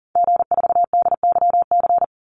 Morse code, heard as a keyed tone, about 41 wpm, 700 Hz. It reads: G4DYC
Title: Call Signs - 40wpm